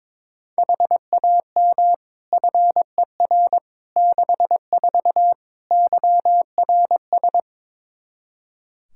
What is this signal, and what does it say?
Morse code, heard as a keyed tone, about 22 wpm, 700 Hz. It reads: HAM FER 64 YRS
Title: Practice QSO - Exchange 3 - More Info - 22wpm